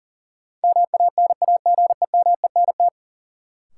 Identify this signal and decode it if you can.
Morse code, heard as a keyed tone, about 40 wpm, 700 Hz. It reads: MANAGEMENT
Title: Sentences from Top 1000 Words - Review Part 4 - New Word Drill - 40wpm